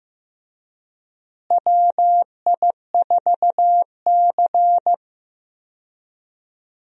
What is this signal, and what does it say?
Morse code, heard as a keyed tone, about 15 wpm, 700 Hz. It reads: WI4C
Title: USA Call Signs - 4 Characters - Repeated 3x - 15wpm